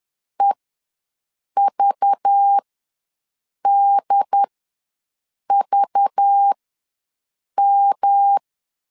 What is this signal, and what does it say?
Morse code, heard as a keyed tone, about 10 wpm, spaced as for 6 wpm, 800 Hz. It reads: EVDVM
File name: LEZIONI / 5 LBVC / 5 LBV ws 10wpm 3645